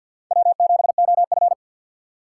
Morse code, heard as a keyed tone, about 48 wpm, 700 Hz. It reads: W7OF
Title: USA Call Signs - 4 Characters - 3x Character Spacing - 50wpm